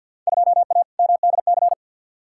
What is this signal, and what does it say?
Morse code, heard as a keyed tone, about 52 wpm, 700 Hz. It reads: 2A MDC
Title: Field Day Exchanges - 50wpm